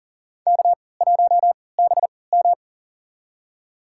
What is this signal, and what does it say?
Morse code, heard as a keyed tone, about 40 wpm, spaced as for 23 wpm, 700 Hz. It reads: K1BM
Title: USA Call Signs - 3x Character Spacing - 40wpm